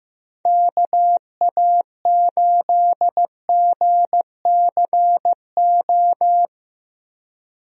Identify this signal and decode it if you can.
Morse code, heard as a keyed tone, about 15 wpm, 700 Hz. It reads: KA8GCO
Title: USA Call Signs - 6 Characters - Repeated 2x - 15wpm